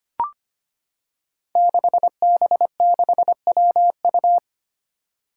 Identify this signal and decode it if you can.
Morse code, heard as a keyed tone, about 25 wpm, 700 Hz. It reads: 6B6WU
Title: Five Challenging Letter-Number Combinations 25wpm